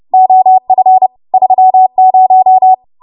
Morse code, heard as a keyed tone, about 30 wpm, 750 Hz. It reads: OF 30
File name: Morse / 30WPM